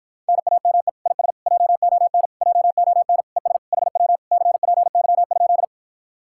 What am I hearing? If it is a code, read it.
Morse code, heard as a keyed tone, about 48 wpm, 700 Hz. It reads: NAME IS JON JON ES HW CPY?